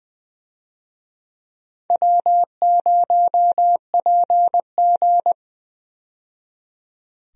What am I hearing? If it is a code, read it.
Morse code, heard as a keyed tone, about 20 wpm, 700 Hz. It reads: W0PG
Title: USA Call Signs - 4 Characters - 20wpm